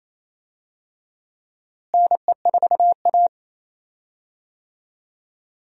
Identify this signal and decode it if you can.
Morse code, heard as a keyed tone, about 28 wpm, 700 Hz. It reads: NE4A
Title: USA Call Signs - 4 Characters - Repeated 3x - 28wpm